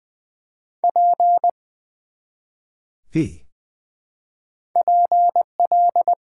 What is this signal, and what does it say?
Morse code, heard as a keyed tone, about 20 wpm, 700 Hz. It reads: P PL
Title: Build A Word - 1 Letter At A Time - 3 to 5-Letter Words - 20wpm